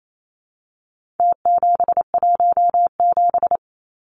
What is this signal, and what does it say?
Morse code, heard as a keyed tone, about 28 wpm, 700 Hz. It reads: T717
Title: Numbers - Contest Style - 4-Digit 28wpm